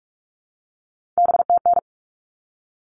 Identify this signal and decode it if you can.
Morse code, heard as a keyed tone, about 44 wpm, 700 Hz. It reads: BTN
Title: All Cut Numbers - 3-Digit 45wpm